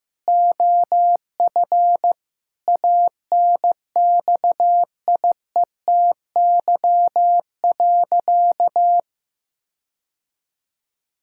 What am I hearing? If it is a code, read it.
Morse code, heard as a keyed tone, about 15 wpm, 700 Hz. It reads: OF ANXIETY.
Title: Sentences from Top 1600 Words 15wpm